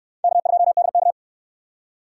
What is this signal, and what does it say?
Morse code, heard as a keyed tone, about 69 wpm, 700 Hz. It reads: K2DK